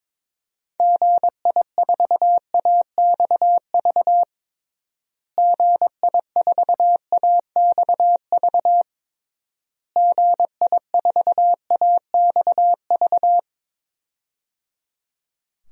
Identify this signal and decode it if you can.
Morse code, heard as a keyed tone, about 22 wpm, 700 Hz. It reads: GI4AXV GI4AXV GI4AXV
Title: DXCC-100 Call Signs - 6 Characters - Repeated 3x - 22wpm